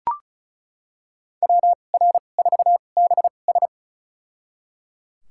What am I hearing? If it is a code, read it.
Morse code, heard as a keyed tone, about 36 wpm, spaced as for 25 wpm, 700 Hz. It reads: WR4BS
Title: USA Call Signs - 5 Characters - 2x Character Spacing - 35wpm